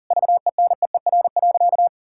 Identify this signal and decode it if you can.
Morse code, heard as a keyed tone, about 40 wpm, 700 Hz. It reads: VENEER.
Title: Pangram Challenge v2 - 40wpm